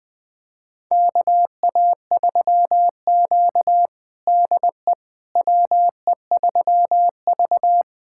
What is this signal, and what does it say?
Morse code, heard as a keyed tone, about 20 wpm, 700 Hz. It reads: KA3Q DE WE3V
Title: USA Callsign DE Callsign - 4 Characters - 20wpm